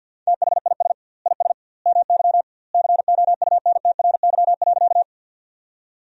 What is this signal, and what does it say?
Morse code, heard as a keyed tone, about 50 wpm, 700 Hz. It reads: THIS IS MY COUNTRY.
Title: 2x Word-Spacing - Sentences from Top 300 Words - 50wpm